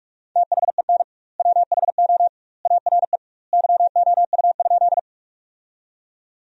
Morse code, heard as a keyed tone, about 46 wpm, 700 Hz. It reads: THEN WHO ARE YOU?
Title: 2x Word-Spacing - Sentences from Top 100 Words - 45wpm